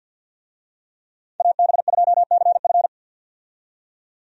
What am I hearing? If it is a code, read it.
Morse code, heard as a keyed tone, about 52 wpm, 700 Hz. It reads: AB2CF
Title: USA Call Signs - 5 Characters - 50wpm